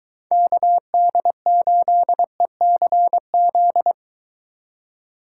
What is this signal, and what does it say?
Morse code, heard as a keyed tone, about 21 wpm, 700 Hz. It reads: KD8ECZ